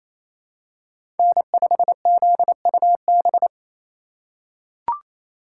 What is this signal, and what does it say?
Morse code, heard as a keyed tone, about 28 wpm, 700 Hz. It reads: N5ZUB
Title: USA Call Signs - 5 Characters - 2x Character Spacing - 28wpm